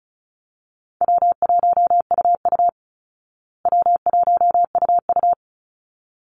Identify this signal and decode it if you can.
Morse code, heard as a keyed tone, about 34 wpm, 700 Hz. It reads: W1UU W1UU
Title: Easy Call Signs Repeated 2x - 35wpm